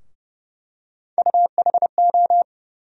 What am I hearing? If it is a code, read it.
Morse code, heard as a keyed tone, about 31 wpm, 700 Hz. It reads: UHO
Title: USA Call Sign Postfixes - 3 Letters 30wpm